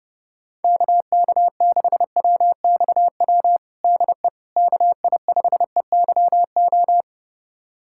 Morse code, heard as a keyed tone, about 30 wpm, 700 Hz. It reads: KK6WXW DE KI5EYO
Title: USA Callsign DE Callsign - 6 Characters - 30wpm